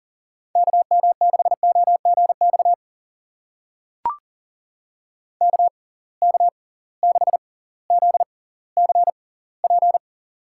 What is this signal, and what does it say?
Morse code, heard as a keyed tone, about 39 wpm, spaced as for 20 wpm, 700 Hz. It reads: KM6OGX KK6ZCP
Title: USA Call Signs - 6 Characters - 6x Character Spacing - 40wpm